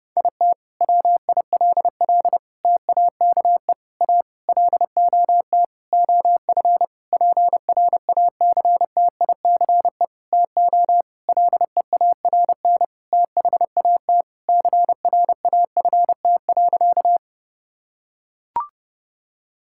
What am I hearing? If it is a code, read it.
Morse code, heard as a keyed tone, about 30 wpm, 700 Hz. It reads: IT WILL TAKE A LOT OF PRACTICE TO LEARN THAT CRAFT.
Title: Sentences from Top 1000 Words - Review Part 6 - 30wpm